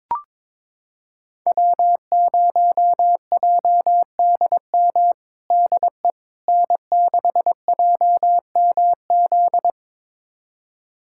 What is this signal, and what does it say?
Morse code, heard as a keyed tone, about 22 wpm, 700 Hz. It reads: W0JDM DE N6JMZ
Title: USA Callsign DE Callsign 22wpm